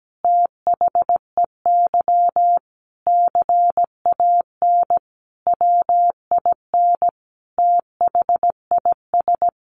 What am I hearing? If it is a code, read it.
Morse code, heard as a keyed tone, about 17 wpm, 700 Hz. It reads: THEY CAN WIN THIS
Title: Sentences from Top 1500 Words 17wpm